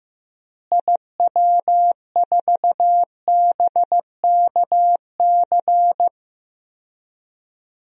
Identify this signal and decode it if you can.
Morse code, heard as a keyed tone, about 15 wpm, 700 Hz. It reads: IW4BKC
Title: DXCC-100 Call Signs - 6 Characters - Repeated 2x - 15wpm